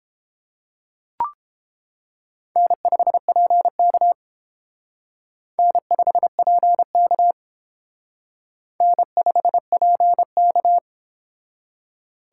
Morse code, heard as a keyed tone, about 28 wpm, 700 Hz. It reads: N5PK N5PK N5PK